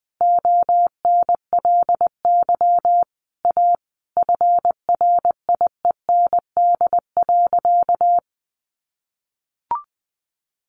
Morse code, heard as a keyed tone, about 20 wpm, 700 Hz. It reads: ONLY A FRIEND.
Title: Sentences from Top 1000 Words 20wpm